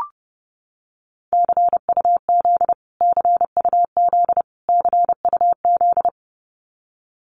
Morse code, heard as a keyed tone, about 30 wpm, 700 Hz. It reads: CUZ CUZ CUZ
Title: Abbreviations Repeated 3x - 30wpm